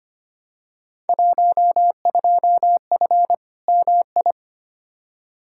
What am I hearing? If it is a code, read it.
Morse code, heard as a keyed tone, about 25 wpm, 700 Hz. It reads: 12F MI
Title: Field Day Exchanges Encore - 25wpm